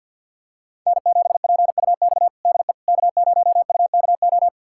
Morse code, heard as a keyed tone, about 53 wpm, 700 Hz. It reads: N7PUX DE K0UKO
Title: USA Callsign DE Callsign - 5 Characters - 50wpm